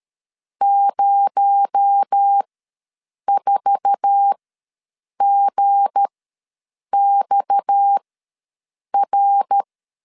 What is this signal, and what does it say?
Morse code, heard as a keyed tone, about 13 wpm, spaced as for 7 wpm, 800 Hz. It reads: 04GXR
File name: LEZIONI / 8 YJ45 / 8 YJ4 ws 12wpm 3249